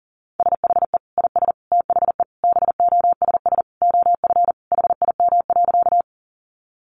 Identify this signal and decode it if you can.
Morse code, heard as a keyed tone, about 40 wpm, 700 Hz. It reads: SHE IS THE BOSS OF HIM.
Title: Sentences from Top 700 Words 40wpm